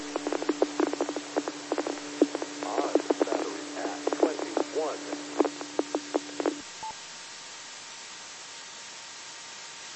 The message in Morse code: none

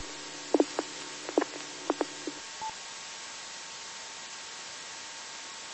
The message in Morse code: none

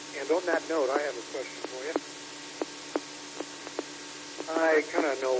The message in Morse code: none

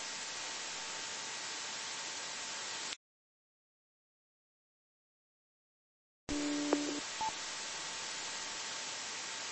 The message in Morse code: none

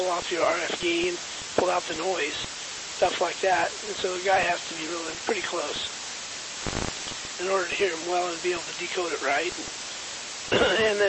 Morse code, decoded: none

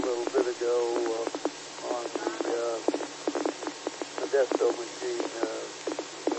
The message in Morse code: none

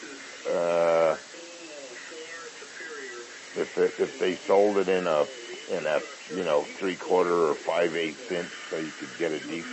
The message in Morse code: none

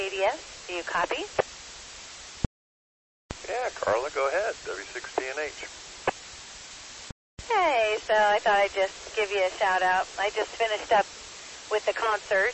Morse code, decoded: none

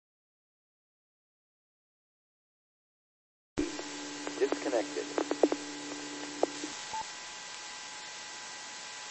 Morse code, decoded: none

